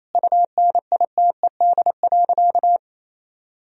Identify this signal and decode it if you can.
Morse code, heard as a keyed tone, about 28 wpm, 700 Hz. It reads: UNITED.
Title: Sentences from Top 1000 Words - Review Part 4 - 28wpm